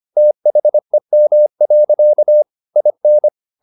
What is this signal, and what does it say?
Morse code, heard as a keyed tone, about 25 wpm, 600 Hz. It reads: THEM. IN
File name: the old man and the sea part 1 25wpm